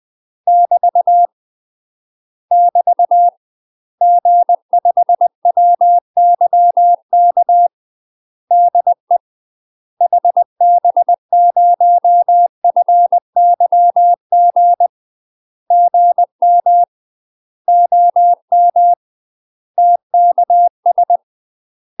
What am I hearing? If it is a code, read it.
Morse code, heard as a keyed tone, about 20 wpm, 700 Hz. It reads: <BT> <BT> G5WYK DE HB0FYG GM OM TKS
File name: UK MASTER QSOs TRAINING / 20WPM / UK MASTER QSOs 0029